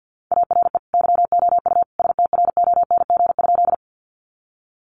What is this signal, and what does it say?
Morse code, heard as a keyed tone, about 50 wpm, 700 Hz. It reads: ARE YOU STRONG?
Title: Sentences from Top 500 Words 50wpm